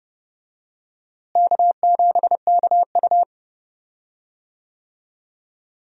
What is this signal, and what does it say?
Morse code, heard as a keyed tone, about 30 wpm, 700 Hz. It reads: K7KU
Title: USA Call Signs - 4 Characters - Repeated 2x - 30wpm